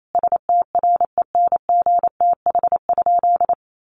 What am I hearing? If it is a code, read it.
Morse code, heard as a keyed tone, about 28 wpm, 700 Hz. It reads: STRENGTH?
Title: Sentences from Top 1300 Words 28wpm